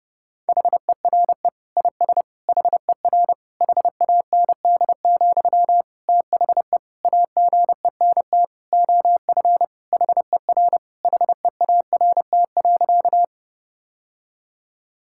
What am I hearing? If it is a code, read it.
Morse code, heard as a keyed tone, about 30 wpm, 700 Hz. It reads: HERE IS HER HAND, THE AGENT OF HER HEART.
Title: Sentences from Top 1600 Words 30wpm